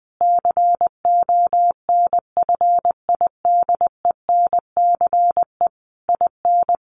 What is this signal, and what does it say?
Morse code, heard as a keyed tone, about 20 wpm, 700 Hz. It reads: CONFIDENCE IN
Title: Sentences from Top 1500 Words 20wpm